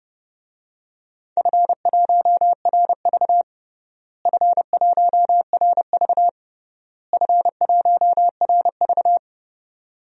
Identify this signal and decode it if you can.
Morse code, heard as a keyed tone, about 32 wpm, 700 Hz. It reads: F1RV F1RV F1RV
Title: DXCC-100 Call Signs - 4 Characters - Repeated 3x - 30wpm